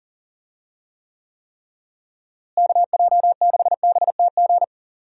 Morse code, heard as a keyed tone, about 40 wpm, 700 Hz. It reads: KJ6BTG
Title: USA Call Signs - 6 Characters - Repeated 3x - 40wpm